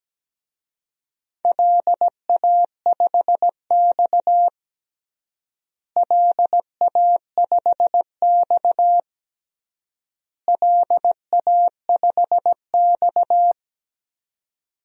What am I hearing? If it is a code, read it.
Morse code, heard as a keyed tone, about 17 wpm, 700 Hz. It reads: LA5X LA5X LA5X
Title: DXCC-100 Call Signs - 4 Characters - Repeated 3x - 17wpm